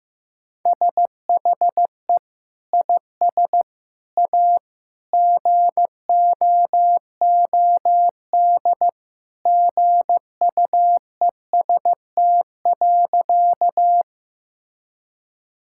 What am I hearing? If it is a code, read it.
Morse code, heard as a keyed tone, about 15 wpm, 700 Hz. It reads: SHE IS A GOOD GUEST.
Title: Sentences from Top 1600 Words 15wpm